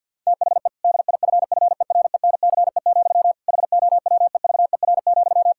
Morse code, heard as a keyed tone, about 50 wpm, 700 Hz. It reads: THE DIFFERENCE, HOWEVER,
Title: Sentences from Top 1000 Words - Review Part 5 - 50wpm